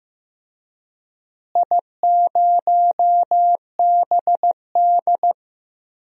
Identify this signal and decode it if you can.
Morse code, heard as a keyed tone, about 15 wpm, 700 Hz. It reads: I0BD
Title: DXCC-100 Call Signs - 4 Characters - Repeated 2x - 15wpm